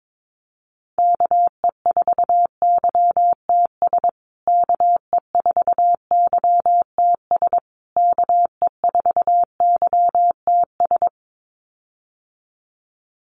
Morse code, heard as a keyed tone, about 22 wpm, 700 Hz. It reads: KE4YTS KE4YTS KE4YTS
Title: USA Call Signs Repeated 3x 22wpm